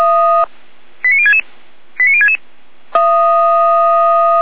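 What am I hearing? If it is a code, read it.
Five-tone selective call (ZVEI), heard as quick runs of five tones, tones of about 70 ms. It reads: 89066 89066
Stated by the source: ZVEI five-tone selective call received on an SDR.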